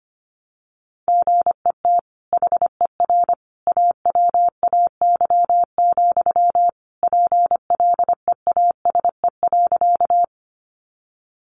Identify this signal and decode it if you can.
Morse code, heard as a keyed tone, about 25 wpm, 700 Hz. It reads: GET HER AWAY, PLEASE.
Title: Sentences from Top 1500 Words 25wpm